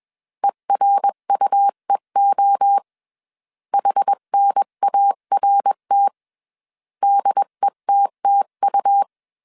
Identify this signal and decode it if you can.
Morse code, heard as a keyed tone, about 21 wpm, 800 Hz. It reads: ERUEO HNART DETTU
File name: CW / DET / 1 3 1500C S B 20wpm 1754